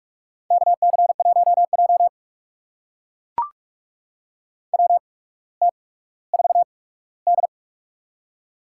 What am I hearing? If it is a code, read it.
Morse code, heard as a keyed tone, about 44 wpm, spaced as for 19 wpm, 700 Hz. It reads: KC1J WT4D
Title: USA Call Signs - 4 Characters - 8x Character Spacing - 45wpm